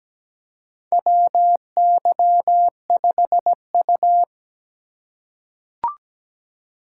Morse code, heard as a keyed tone, about 17 wpm, 700 Hz. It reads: WY5U
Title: USA Call Signs - 4 Characters - 4x Character Spacing - 17wpm